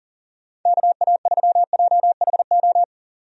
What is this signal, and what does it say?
Morse code, heard as a keyed tone, about 39 wpm, 700 Hz. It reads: KA3JHO
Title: USA Call Signs - 6 Characters - 40wpm